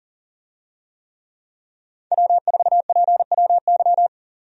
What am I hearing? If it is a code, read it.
Morse code, heard as a keyed tone, about 40 wpm, 700 Hz. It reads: W4PWY
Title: USA Call Signs - 5 Characters - 40wpm